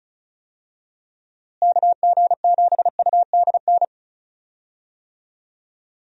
Morse code, heard as a keyed tone, about 34 wpm, 700 Hz. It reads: KG7UDN
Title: USA Call Signs - 6 Characters - Repeated 3x - 35wpm